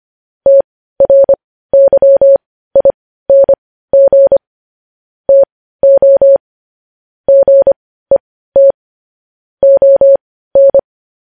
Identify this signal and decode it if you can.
Morse code, heard as a keyed tone, about 25 wpm, spaced as for 15 wpm, 550 Hz. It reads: TRYING TO GET ON
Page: news-headlines-in-morse-code-at-15-wpm